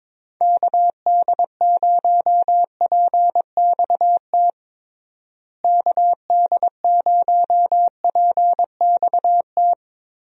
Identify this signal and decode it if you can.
Morse code, heard as a keyed tone, about 22 wpm, 700 Hz. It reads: KD0PXT KD0PXT
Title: USA Call Signs - 6 Characters - Repeated 2x - 22wpm